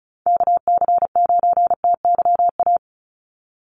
Morse code, heard as a keyed tone, about 34 wpm, 700 Hz. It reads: KC9TYA
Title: NA SOTA Chaser Callsigns 35wpm